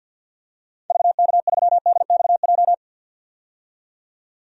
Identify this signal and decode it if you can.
Morse code, heard as a keyed tone, about 52 wpm, 700 Hz. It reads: VK3DXJ
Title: DXCC-100 Call Signs - 6 Characters - 50wpm